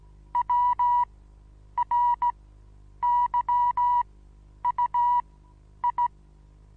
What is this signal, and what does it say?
Morse code, heard as a keyed tone, about 17 wpm, spaced as for 9 wpm, 970 Hz. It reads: WRYUI
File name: Onlinelehrgang / morsen / DJ4UF Morsen25